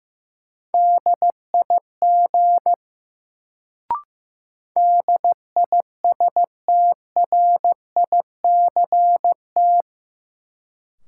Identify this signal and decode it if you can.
Morse code, heard as a keyed tone, about 15 wpm, 700 Hz. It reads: DIG DISTRICT
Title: Sentences from Top 1600 Words - New Word Drill Alt - 15wpm